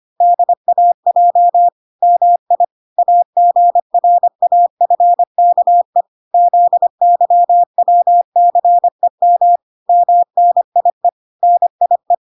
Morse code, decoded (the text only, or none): DAJ MI AGRAFKE ZYWCEM MNIE NIE